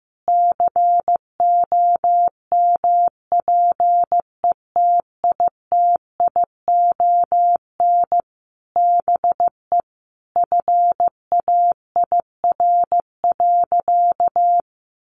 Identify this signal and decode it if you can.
Morse code, heard as a keyed tone, about 15 wpm, 700 Hz. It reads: COMPETITION BE FAIR.
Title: Sentences from Top 1300 Words 15wpm